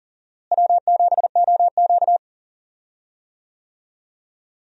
Morse code, heard as a keyed tone, about 40 wpm, 700 Hz. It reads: W7OQ
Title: USA Call Signs - 4 Characters - Repeated 3x - 40wpm